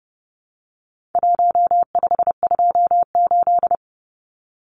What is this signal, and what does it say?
Morse code, heard as a keyed tone, about 30 wpm, 700 Hz. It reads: 1528
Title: Numbers - Contest Style - 4-Digit 30wpm